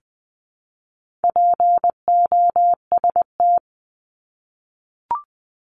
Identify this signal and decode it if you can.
Morse code, heard as a keyed tone, about 20 wpm, 700 Hz. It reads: POST
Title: 260 Lesson 42e - Words 20-20wpm